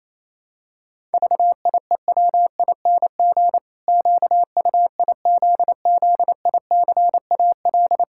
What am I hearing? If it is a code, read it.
Morse code, heard as a keyed tone, about 28 wpm, 700 Hz. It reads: VIEWING QUIZZICAL
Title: Pangram Challenge v2 - 28wpm